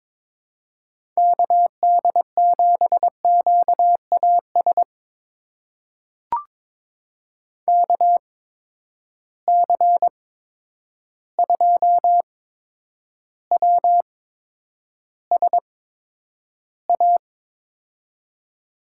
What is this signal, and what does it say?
Morse code, heard as a keyed tone, about 22 wpm, spaced as for 9 wpm, 700 Hz. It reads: KD7QAS KC2WSA
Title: USA Call Signs - 6 Characters - 8x Character Spacing - 22wpm